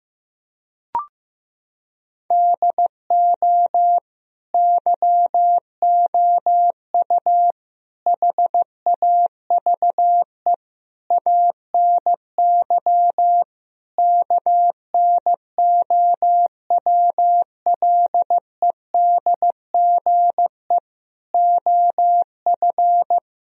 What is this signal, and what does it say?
Morse code, heard as a keyed tone, about 15 wpm, 700 Hz. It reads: DO YOU HAVE ANY KNOWLEDGE OF
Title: Sentences from Top 1000 Words - Review Part 3 - 15wpm